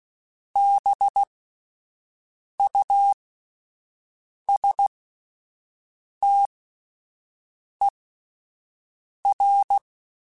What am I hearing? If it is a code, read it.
Morse code, heard as a keyed tone, about 16 wpm, spaced as for 5 wpm, 800 Hz. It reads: BUSTER